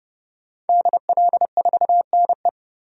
Morse code, heard as a keyed tone, about 31 wpm, 700 Hz. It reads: DL4NE
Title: DXCC-100 Call Signs - 5 Characters - 30wpm